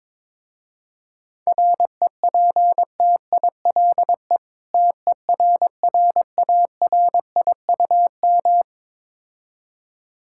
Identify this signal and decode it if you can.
Morse code, heard as a keyed tone, about 22 wpm, 700 Hz. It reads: REPTILE TERRARIUM
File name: Difficult Words with Hints - 22wpm